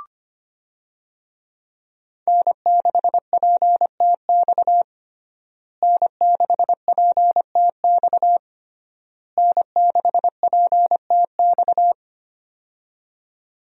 Morse code, heard as a keyed tone, about 25 wpm, 700 Hz. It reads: N6PTX N6PTX N6PTX
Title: USA Call Signs - 5 Characters - Repeated 3x - 25wpm